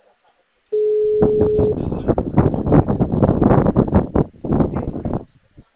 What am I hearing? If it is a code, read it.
Ringback tone (Europe, 425 Hz), with noise.